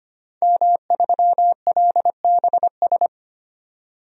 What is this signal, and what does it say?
Morse code, heard as a keyed tone, about 25 wpm, 700 Hz. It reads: M3LBS
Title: DXCC-100 Call Signs - 5 Characters - 25wpm